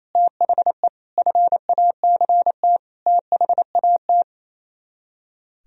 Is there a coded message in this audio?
Yes: Morse code from 0.1 to 4.2 s, heard as a keyed tone, about 36 wpm, 690 Hz. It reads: THE FACT THAT